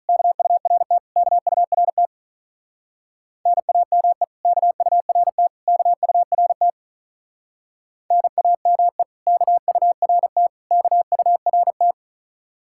Morse code, heard as a keyed tone, about 43 wpm, 700 Hz. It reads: KURT KURT NAME KURT KURT NAME KURT KURT
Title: Speed-Racing Common QSO Phrases - 35wpm